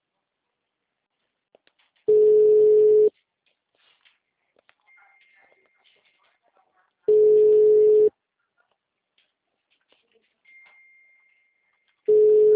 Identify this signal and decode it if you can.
Ringback tone (Europe, 425 Hz), with speech.